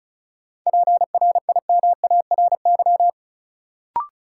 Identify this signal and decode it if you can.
Morse code, heard as a keyed tone, about 36 wpm, 700 Hz. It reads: PRIMARY
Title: Sentences from Top 1100 Words - New Word Drill Alt - 35wpm